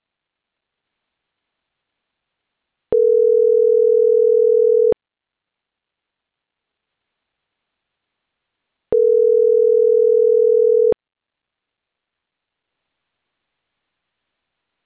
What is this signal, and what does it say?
Ringback tone (North America), 440 + 480 Hz.